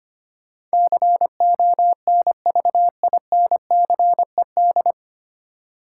Morse code, heard as a keyed tone, about 25 wpm, 700 Hz. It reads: CONVINCED
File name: practice / 25wpm / 9 Letter Words - First 3 Letters Given - 25wpm